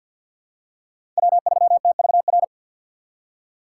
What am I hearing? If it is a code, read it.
Morse code, heard as a keyed tone, about 47 wpm, 700 Hz. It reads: W3TVR
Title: USA Call Signs - 5 Characters - 4x Character Spacing - 50wpm